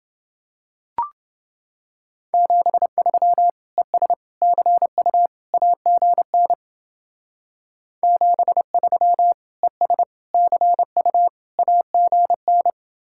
Morse code, heard as a keyed tone, about 29 wpm, 700 Hz. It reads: 73 ES CU AGN 73 ES CU AGN